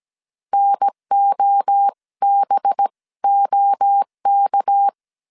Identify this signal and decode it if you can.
Morse code, heard as a keyed tone, about 17 wpm, spaced as for 15 wpm, 800 Hz. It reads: NOBOK
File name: CW / DET / 1 6 1500C S A 16wpm 2635